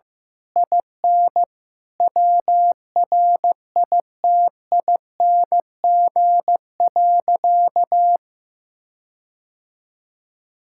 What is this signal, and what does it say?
Morse code, heard as a keyed tone, about 15 wpm, 700 Hz. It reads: IN WRITING.
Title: Sentences from Top 1600 Words 15wpm